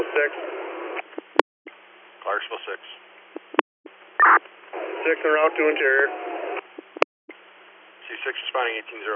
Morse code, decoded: none

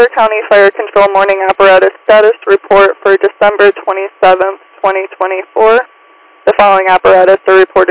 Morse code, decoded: none

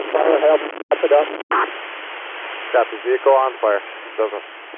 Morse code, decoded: none

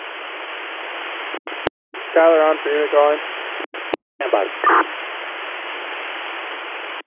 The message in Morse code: none

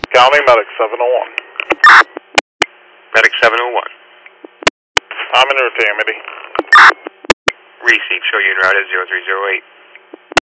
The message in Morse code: none

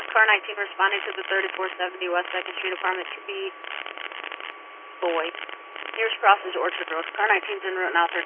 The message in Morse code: none

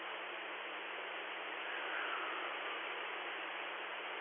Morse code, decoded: none

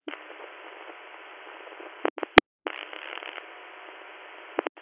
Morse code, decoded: none

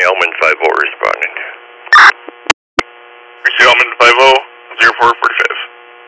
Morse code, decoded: none

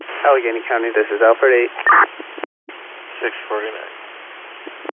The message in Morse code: none

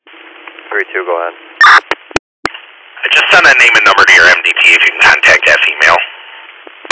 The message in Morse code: none

